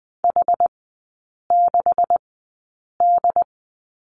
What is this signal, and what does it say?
Morse code, heard as a keyed tone, about 20 wpm, spaced as for 8 wpm, 700 Hz. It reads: H6D
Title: 205 Lesson 33e - Letter j - Three Characters 20wpm